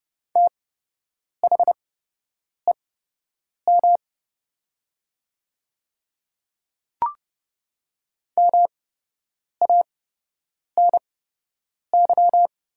Morse code, heard as a keyed tone, about 30 wpm, spaced as for 8 wpm, 700 Hz. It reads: THEM MANY
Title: 8x Character Spacing - Top 100 Words - 30wpm